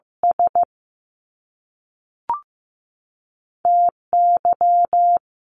Morse code, heard as a keyed tone, about 15 wpm, 700 Hz. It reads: S T Y
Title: Top Two Letter Combinations in Words 15wpm